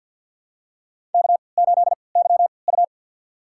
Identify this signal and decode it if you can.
Morse code, heard as a keyed tone, about 46 wpm, spaced as for 27 wpm, 700 Hz. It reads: K8YU
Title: USA Call Signs - 4 Characters - 3x Character Spacing - 50wpm